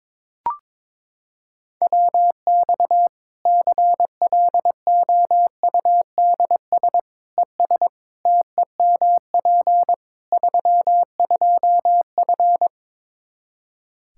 Morse code, heard as a keyed tone, about 22 wpm, 700 Hz. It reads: WX CLOUDS ES TEMP 32F
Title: Practice QSO - Exchange 2 - Info - 22wpm